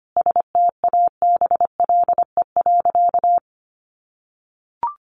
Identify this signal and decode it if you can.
Morse code, heard as a keyed tone, about 25 wpm, 700 Hz. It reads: STABLE.
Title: Sentences from Top 1500 Words 25wpm